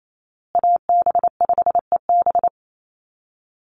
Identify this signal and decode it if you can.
Morse code, heard as a keyed tone, about 28 wpm, 700 Hz. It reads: AB5EB
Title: Call Signs Repeated 2x - 28wpm